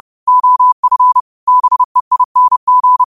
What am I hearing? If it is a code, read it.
Morse code, heard as a keyed tone, about 30 wpm, 1000 Hz. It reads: OF BEING